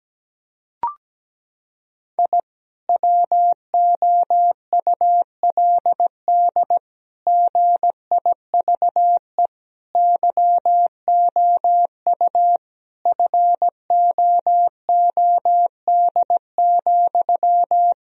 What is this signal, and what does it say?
Morse code, heard as a keyed tone, about 17 wpm, 700 Hz. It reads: I WOULD GIVE YOU FOOD,
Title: Sentences from Top 1000 Words - Review Part 1 - 17wpm